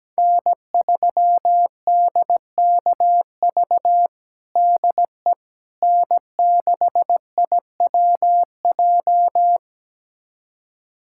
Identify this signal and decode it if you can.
Morse code, heard as a keyed tone, about 17 wpm, 700 Hz. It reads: N3DKV DE N6IWJ
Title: USA Callsign DE Callsign - 5 Characters - 17wpm